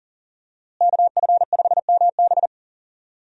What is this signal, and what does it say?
Morse code, heard as a keyed tone, about 40 wpm, 700 Hz. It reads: KF5MB
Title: USA Call Signs - 5 Characters - Repeated 2x - 40wpm